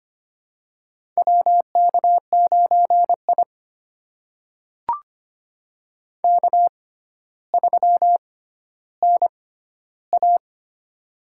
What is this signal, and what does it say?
Morse code, heard as a keyed tone, about 25 wpm, spaced as for 13 wpm, 700 Hz. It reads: WK9I K3NA
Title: USA Call Signs - 4 Characters - 6x Character Spacing - 25wpm